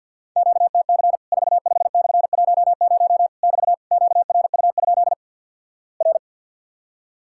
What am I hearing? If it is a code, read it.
Morse code, heard as a keyed tone, about 48 wpm, 700 Hz. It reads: QTC 45/10 <BT> QRU? R
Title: Worked All Europe DX Contest - QTC Exchange with Repeats - 50wpm